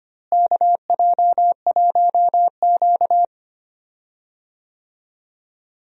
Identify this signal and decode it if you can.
Morse code, heard as a keyed tone, about 25 wpm, 700 Hz. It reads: KJ1Q
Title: USA Call Signs - 4 Characters - Repeated 2x - 25wpm